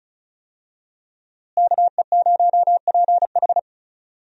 Morse code, heard as a keyed tone, about 36 wpm, 700 Hz. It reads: KE0PH
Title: USA Call Signs - 5 Characters - Repeated 3x - 35wpm